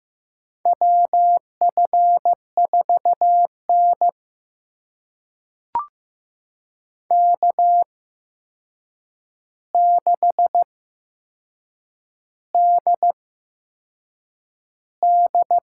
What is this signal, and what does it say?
Morse code, heard as a keyed tone, about 15 wpm, spaced as for 6 wpm, 700 Hz. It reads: WF4N K6DD
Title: USA Call Signs - 4 Characters - 8x Character Spacing - 15wpm